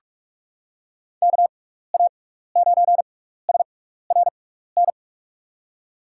Morse code, heard as a keyed tone, about 47 wpm, spaced as for 16 wpm, 700 Hz. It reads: KA9SRN
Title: USA Call Signs - 6 Characters - 6x Character Spacing - 45wpm